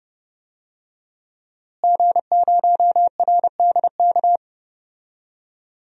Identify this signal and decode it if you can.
Morse code, heard as a keyed tone, about 30 wpm, 700 Hz. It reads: G0RDK
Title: DXCC-100 Call Signs - 5 Characters - Repeated 2x - 30wpm